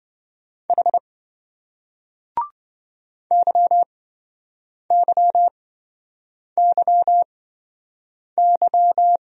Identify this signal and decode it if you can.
Morse code, heard as a keyed tone, about 27 wpm, spaced as for 7 wpm, 700 Hz. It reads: H YYYY